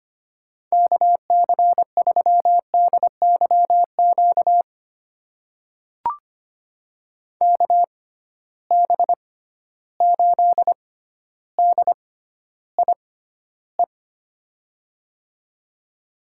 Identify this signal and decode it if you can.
Morse code, heard as a keyed tone, about 25 wpm, spaced as for 13 wpm, 700 Hz. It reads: KC3DYQ KB8DIE
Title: USA Call Signs - 6x Character Spacing - 25wpm